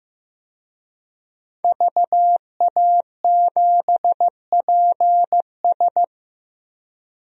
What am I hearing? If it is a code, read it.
Morse code, heard as a keyed tone, about 15 wpm, 700 Hz. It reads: VA7PS
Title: DXCC-100 Call Signs - 5 Characters - 15wpm